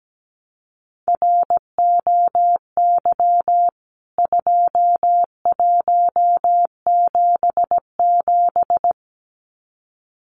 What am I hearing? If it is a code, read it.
Morse code, heard as a keyed tone, about 17 wpm, 700 Hz. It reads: ROY 2177
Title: CWT Exchange 17wpm